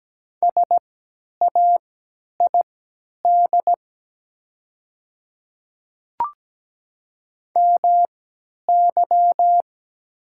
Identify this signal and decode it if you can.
Morse code, heard as a keyed tone, about 17 wpm, spaced as for 10 wpm, 700 Hz. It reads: SAID MY
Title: 3x Character Spacing - Top 100 Words - 17wpm